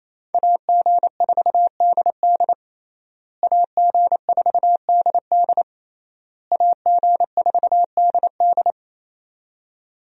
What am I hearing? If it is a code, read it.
Morse code, heard as a keyed tone, about 27 wpm, 700 Hz. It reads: AG4DD AG4DD AG4DD
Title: USA Call Signs - 5 Characters - Repeated 3x - 28wpm